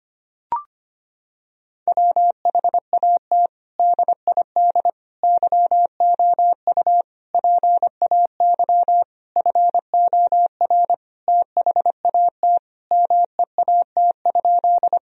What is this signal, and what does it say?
Morse code, heard as a keyed tone, about 25 wpm, 700 Hz. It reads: WHAT DID YOU PAY FOR THAT MEAT?
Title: Sentences from Top 1000 Words - Review Part 3 - 25wpm